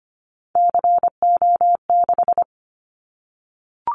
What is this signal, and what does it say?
Morse code, heard as a keyed tone, about 25 wpm, 700 Hz. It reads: CO6
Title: Callsign Prefixes for Top 100 DXCC - 25wpm